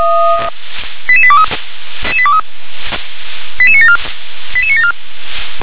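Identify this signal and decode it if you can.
Five-tone selective call (ZVEI), heard as quick runs of five tones, tones of about 70 ms. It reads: 88823 88974 88974
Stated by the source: Five-tone ZVEI selective call received on SDR.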